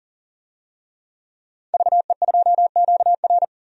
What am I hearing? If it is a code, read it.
Morse code, heard as a keyed tone, about 40 wpm, 700 Hz. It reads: VE2QR
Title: DXCC-100 Call Signs - 5 Characters - 40wpm